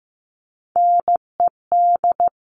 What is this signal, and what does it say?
Morse code, heard as a keyed tone, about 15 wpm, 700 Hz. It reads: NED
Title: CWT Exchange 15wpm